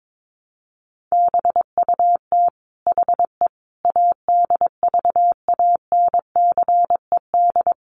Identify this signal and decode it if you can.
Morse code, heard as a keyed tone, about 22 wpm, 700 Hz. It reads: BUT HE ADVANCED
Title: Sentences from Top 1200 Words 22wpm